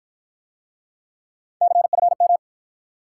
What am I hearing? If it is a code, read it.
Morse code, heard as a keyed tone, about 52 wpm, 700 Hz. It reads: CFM